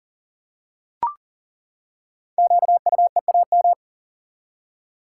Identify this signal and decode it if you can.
Morse code, heard as a keyed tone, about 42 wpm, 700 Hz. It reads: QUEAM
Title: Code Groups - 5 Letters - 40wpm